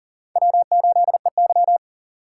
Morse code, heard as a keyed tone, about 40 wpm, 700 Hz. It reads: W8EY
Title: USA Call Signs - 4 Characters - 6x Character Spacing - 40wpm